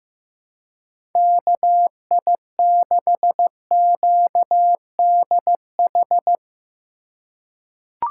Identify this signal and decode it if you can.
Morse code, heard as a keyed tone, about 15 wpm, 700 Hz. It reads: KI6QDH
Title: K1USN SST QSO Practice - 15wpm